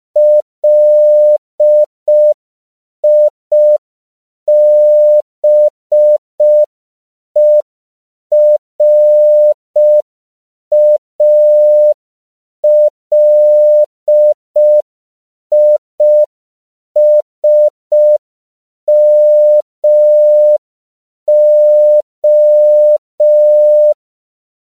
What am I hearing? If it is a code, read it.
Morse code, heard as a keyed tone, about 5 wpm, 600 Hz. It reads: LIBERALISMO
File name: CW / 5PPM